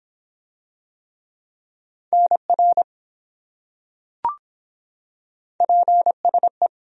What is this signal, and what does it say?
Morse code, heard as a keyed tone, about 23 wpm, 700 Hz. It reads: NR PSE